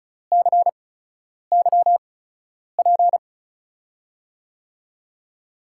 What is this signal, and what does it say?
Morse code, heard as a keyed tone, about 36 wpm, spaced as for 10 wpm, 700 Hz. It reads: CYP
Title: USA Call Signs - 8x Character Spacing - 35wpm